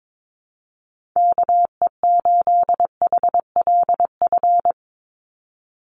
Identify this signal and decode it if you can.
Morse code, heard as a keyed tone, about 22 wpm, 700 Hz. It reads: KE8HLF
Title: USA Call Signs 22wpm